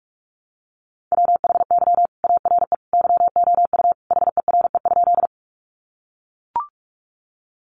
Morse code, heard as a keyed tone, about 44 wpm, 700 Hz. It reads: WHY ARE YOU HERE?
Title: Sentences from Top 200 Words 45wpm